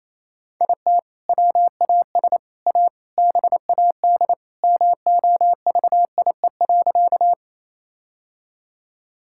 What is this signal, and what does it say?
Morse code, heard as a keyed tone, about 28 wpm, 700 Hz. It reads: IT WAS A BAD MOVIE.
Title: Sentences from Top 1000 Words - Review Part 5 - 28wpm